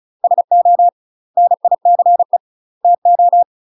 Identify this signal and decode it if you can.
Morse code, heard as a keyed tone, about 33 wpm, 700 Hz. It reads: SO NICE TO
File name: UK MASTER QSOs TRAINING / 35WPM / UK MASTER QSOs 0018